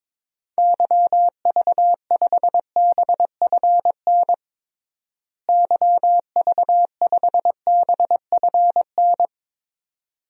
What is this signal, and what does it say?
Morse code, heard as a keyed tone, about 22 wpm, 700 Hz. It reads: YV5BFN YV5BFN
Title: DXCC-100 Call Signs - 6 Characters - Repeated 2x - 22wpm